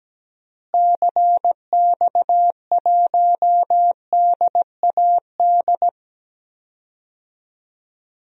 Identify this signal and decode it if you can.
Morse code, heard as a keyed tone, about 17 wpm, 700 Hz. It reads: CX1DAD
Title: DXCC-100 Call Signs - 6 Characters - Repeated 2x - 17wpm